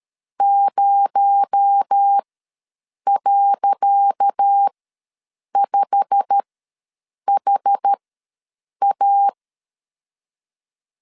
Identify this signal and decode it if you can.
Morse code, heard as a keyed tone, about 13 wpm, spaced as for 7 wpm, 800 Hz. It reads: 0.5HA
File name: LEZIONI / 10 SIGNS / 10D COMA ws 12wpm 3618